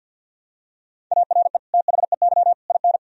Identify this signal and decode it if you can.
Morse code, heard as a keyed tone, about 50 wpm, 700 Hz. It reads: ARE THEY IN
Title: Sentences from Top 1000 Words - Review Part 1 - 50wpm